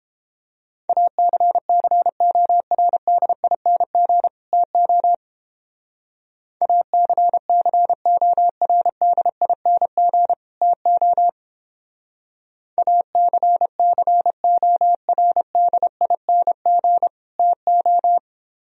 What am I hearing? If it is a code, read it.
Morse code, heard as a keyed tone, about 31 wpm, 700 Hz. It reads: ACCORDING TO ACCORDING TO ACCORDING TO